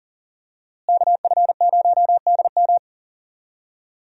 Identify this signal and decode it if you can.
Morse code, heard as a keyed tone, about 40 wpm, 700 Hz. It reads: KF0DM
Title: USA Call Signs - 5 Characters - Repeated 2x - 40wpm